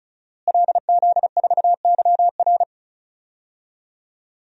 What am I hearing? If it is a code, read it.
Morse code, heard as a keyed tone, about 34 wpm, 700 Hz. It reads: LZ4YR
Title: DXCC-100 Call Signs - 5 Characters - Repeated 3x - 35wpm